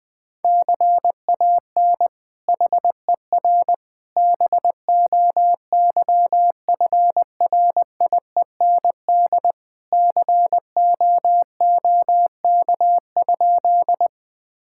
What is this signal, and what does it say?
Morse code, heard as a keyed tone, about 20 wpm, 700 Hz. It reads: CAN HER BOYFRIEND COOK?
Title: Sentences from Top 1000 Words - Review Part 5 - 20wpm